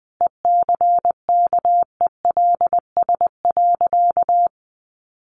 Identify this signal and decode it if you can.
Morse code, heard as a keyed tone, about 20 wpm, 700 Hz. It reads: ECKELS.
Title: A Sound Of Thunder 20wpm